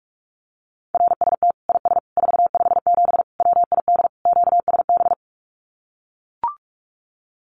Morse code, heard as a keyed tone, about 44 wpm, 700 Hz. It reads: RST IS 457 WID QSB
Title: Common QSO Phrases 45wpm